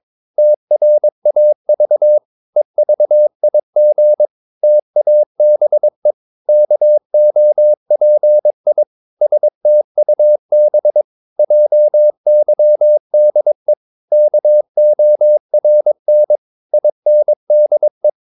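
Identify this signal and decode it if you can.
Morse code, heard as a keyed tone, about 22 wpm, 600 Hz. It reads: TRAV EVIG TABE KOPI STUB JYDE KORN INDE
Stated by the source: Morse practice of short Danish words at 22 wpm, from CW med Gnister.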